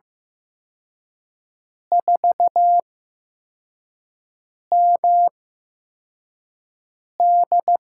Morse code, heard as a keyed tone, about 15 wpm, spaced as for 4 wpm, 700 Hz. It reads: 4MD
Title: USA Call Signs - 4 Characters - 8x Character Spacing - 15wpm